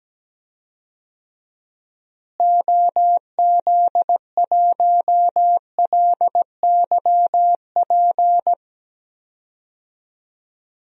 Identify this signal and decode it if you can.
Morse code, heard as a keyed tone, about 17 wpm, 700 Hz. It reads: OZ1LYP
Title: DXCC-100 Call Signs - 6 Characters - Repeated 3x - 17wpm